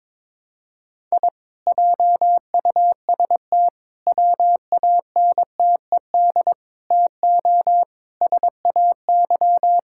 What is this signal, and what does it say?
Morse code, heard as a keyed tone, about 21 wpm, 700 Hz. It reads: I JUST WANTED TO SAY
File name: practice / 22wpm / Sentences with 4-Word Phrases - 22wpm